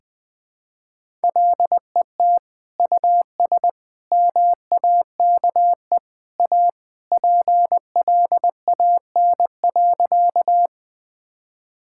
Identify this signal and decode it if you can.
Morse code, heard as a keyed tone, about 20 wpm, 700 Hz. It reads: LET US MAKE A PLAN.
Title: Sentences with 2-Word Phrases - 20wpm